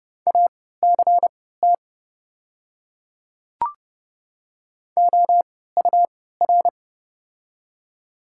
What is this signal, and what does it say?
Morse code, heard as a keyed tone, about 30 wpm, spaced as for 17 wpm, 700 Hz. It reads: ACT OUR
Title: 3x Character Spacing - Top 300 Words - 30wpm